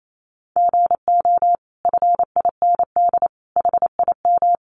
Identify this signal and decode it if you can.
Morse code, heard as a keyed tone, about 28 wpm, 700 Hz. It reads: GO FIND HIM
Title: Sentences from Top 1100 Words 28wpm